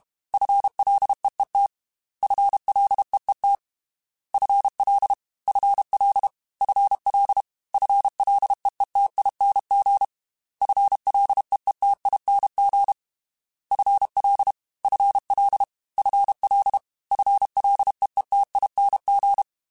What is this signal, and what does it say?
Morse code, heard as a keyed tone, about 32 wpm, 800 Hz. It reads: FLEET FLEET FL FL FL FLEETING FLEETING FL FL FL FLEETING
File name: FL / FL mp3s / FL Prefixes 3x2 Twice @30